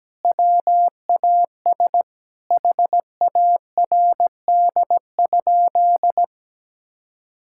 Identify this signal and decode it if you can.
Morse code, heard as a keyed tone, about 17 wpm, 700 Hz. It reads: WAS HARD?
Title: Sentences from Top 1000 Words - Review Part 5 - 17wpm